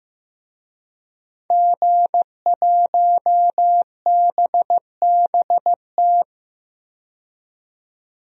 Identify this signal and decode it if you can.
Morse code, heard as a keyed tone, about 15 wpm, 700 Hz. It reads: G1BBT
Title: DXCC-100 Call Signs - 5 Characters - 15wpm